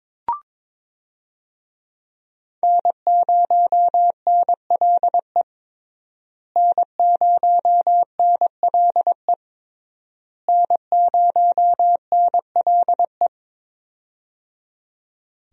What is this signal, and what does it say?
Morse code, heard as a keyed tone, about 22 wpm, 700 Hz. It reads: N0NLE N0NLE N0NLE
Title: USA Call Signs - 5 Characters - Repeated 3x - 22wpm